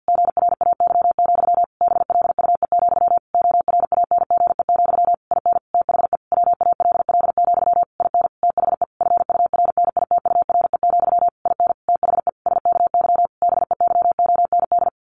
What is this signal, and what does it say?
Morse code, heard as a keyed tone, about 50 wpm, 700 Hz. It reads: GRAY, BLUE, ORANGE, IN THE WALL, IN THE FURNITURE, IN THE SKY BEYOND
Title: A Sound Of Thunder 50wpm